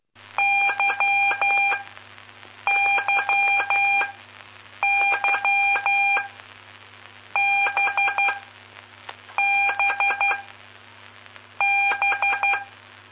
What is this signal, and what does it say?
Morse code, heard as a keyed tone, about 12 wpm, spaced as for 6 wpm, 2400 Hz, with noise. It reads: YYY BBB
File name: pp5bar / aulas de telegrafia / Aula-10